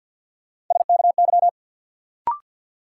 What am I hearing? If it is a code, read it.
Morse code, heard as a keyed tone, about 53 wpm, 700 Hz. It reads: SKY
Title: Sentences from Top 1200 Words - New Word Drill Alt - 50wpm